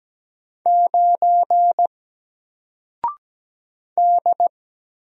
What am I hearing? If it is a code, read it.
Morse code, heard as a keyed tone, about 17 wpm, 700 Hz. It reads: 9 D
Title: Single Letter-Number Alt 17wpm